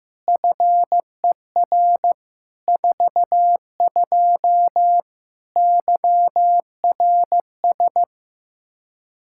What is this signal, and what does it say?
Morse code, heard as a keyed tone, about 15 wpm, 700 Hz. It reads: FER 42 YRS
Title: Practice QSO - Exchange 3 - More Info - 15wpm